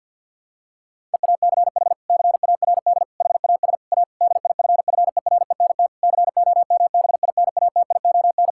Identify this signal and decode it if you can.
Morse code, heard as a keyed tone, about 52 wpm, 700 Hz. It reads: EACH CARD HAS A DIFFERENT COMBINATION
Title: Sentences from Top 1000 Words - Review Part 6 - 50wpm